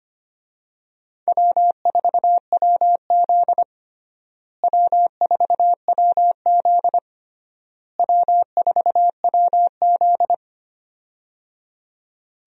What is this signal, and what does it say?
Morse code, heard as a keyed tone, about 25 wpm, 700 Hz. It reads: W4WZ W4WZ W4WZ
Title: USA Call Signs - 4 Characters - Repeated 3x - 25wpm